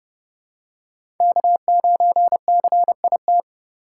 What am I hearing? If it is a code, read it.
Morse code, heard as a keyed tone, about 32 wpm, 700 Hz. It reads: K9CIT
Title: USA Call Signs - 5 Characters - 2x Character Spacing - 30wpm